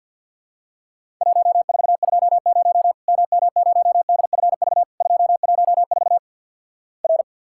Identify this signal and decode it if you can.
Morse code, heard as a keyed tone, about 50 wpm, 700 Hz. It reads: 1420 MM0DFV 214 R
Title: Worked All Europe DX Contest - QTC Exchange - 50wpm